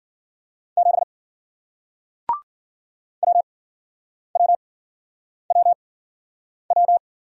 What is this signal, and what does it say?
Morse code, heard as a keyed tone, about 49 wpm, spaced as for 9 wpm, 700 Hz. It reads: 7 WWWW
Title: Speed-Racing Single Letter-Number 40wpm